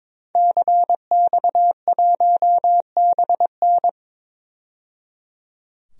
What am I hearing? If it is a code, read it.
Morse code, heard as a keyed tone, about 22 wpm, 700 Hz. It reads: CX1BN
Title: DXCC-100 Call Signs - 5 Characters - 22wpm